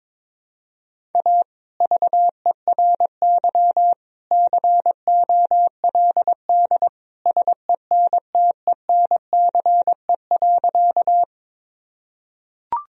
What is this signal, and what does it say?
Morse code, heard as a keyed tone, about 22 wpm, 700 Hz. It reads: A VERY COLD SENTENCE.
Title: Sentences from Top 1000 Words - Review Part 1 - 22wpm